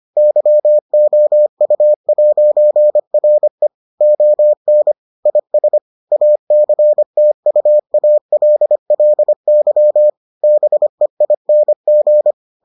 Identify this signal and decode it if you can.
Morse code, heard as a keyed tone, about 25 wpm, 600 Hz. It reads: YOU'RE ON IS ACTUALLY BEING